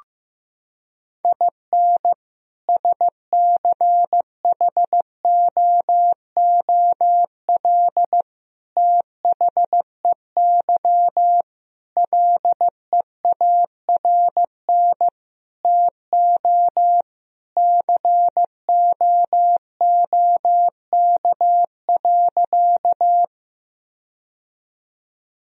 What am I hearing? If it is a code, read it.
Morse code, heard as a keyed tone, about 15 wpm, 700 Hz. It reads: IN SCHOOL THEY LEARN TO COOK.
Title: Sentences from Top 1000 Words - Review Part 4 - 15wpm